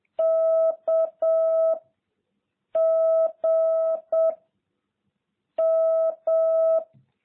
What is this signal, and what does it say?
Morse code, heard as a keyed tone, about 7 wpm, 660 Hz. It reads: K G M